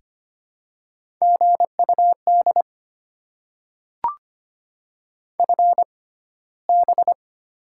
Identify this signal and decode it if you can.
Morse code, heard as a keyed tone, about 25 wpm, spaced as for 15 wpm, 700 Hz. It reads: GUD FB
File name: Top 20 QSO Elements - 6x Character Spacing - 25wpm